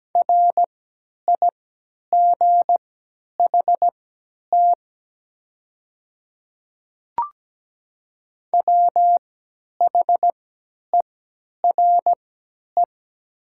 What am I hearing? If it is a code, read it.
Morse code, heard as a keyed tone, about 17 wpm, spaced as for 10 wpm, 700 Hz. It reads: RIGHT WHERE
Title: 3x Character Spacing - Top 200 Words - 17wpm